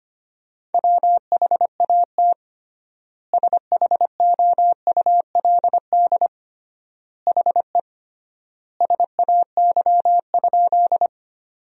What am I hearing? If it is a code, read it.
Morse code, heard as a keyed tone, about 25 wpm, 700 Hz. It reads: WHAT SHOULD HE SAY?
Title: 3x Word-Spacing - Sentences from Top 300 Words - 25wpm